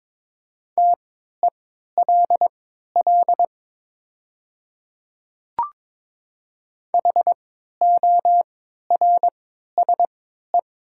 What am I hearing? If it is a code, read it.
Morse code, heard as a keyed tone, about 22 wpm, spaced as for 13 wpm, 700 Hz. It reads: TELL HORSE
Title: 3x Character Spacing - Top 300 Words - 22wpm